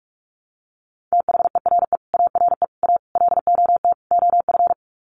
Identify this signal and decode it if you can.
Morse code, heard as a keyed tone, about 44 wpm, 700 Hz. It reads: THERE ARE A LOT OF
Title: Sets of 5 Words 45wpm